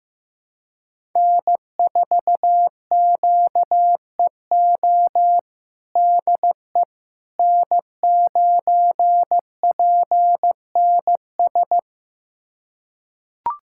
Morse code, heard as a keyed tone, about 15 wpm, 700 Hz. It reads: N4QEO DE N9PNS
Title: USA Callsign DE Callsign - 5 Characters - 15wpm